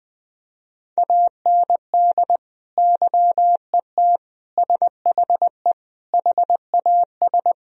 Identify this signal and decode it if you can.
Morse code, heard as a keyed tone, about 20 wpm, 700 Hz. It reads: AND YET SHE HAS
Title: Sentences from Top 1000 Words - Review Part 1 - 20wpm